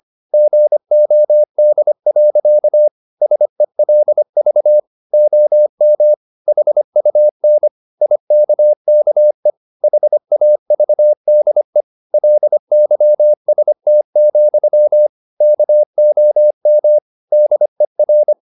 Morse code, heard as a keyed tone, about 25 wpm, 600 Hz. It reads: GOD. SELV OM HUN IKKE HAVDE LYST, KOM DER
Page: Ferie 25 wpm